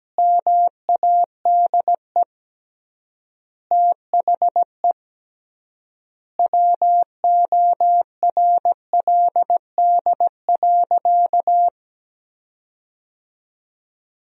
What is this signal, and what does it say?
Morse code, heard as a keyed tone, about 17 wpm, 700 Hz. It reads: MADE THE WORLD.
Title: 3x Word-Spacing - Sentences from Top 200 Words - 17wpm